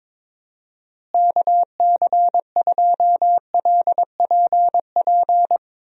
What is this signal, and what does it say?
Morse code, heard as a keyed tone, about 22 wpm, 700 Hz. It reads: KC2LPP
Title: USA Call Signs - 6 Characters - Repeated 3x - 22wpm